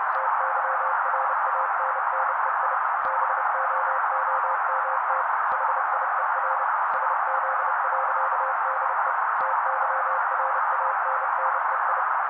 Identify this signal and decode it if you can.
Morse code, heard as a keyed tone, about 30 wpm, 530 Hz. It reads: TYRANNEI BOOMT HIER SZ.DE TYRANNEI
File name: daily / 29wpm noise